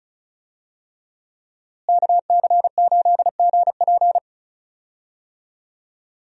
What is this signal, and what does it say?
Morse code, heard as a keyed tone, about 36 wpm, 700 Hz. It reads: KC8GP
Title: USA Call Signs - 5 Characters - Repeated 3x - 35wpm